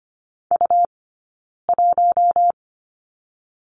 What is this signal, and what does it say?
Morse code, heard as a keyed tone, about 25 wpm, 700 Hz. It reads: U 1
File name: Lesson 17d - Letter u - Two Characters 25wpm